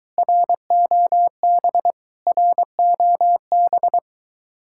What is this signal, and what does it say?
Morse code, heard as a keyed tone, about 20 wpm, 700 Hz. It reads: ROB ROB